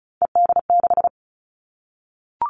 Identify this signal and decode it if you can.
Morse code, heard as a keyed tone, about 36 wpm, 700 Hz. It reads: ED6
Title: Callsign Prefixes for Top 50 DXCC - 35wpm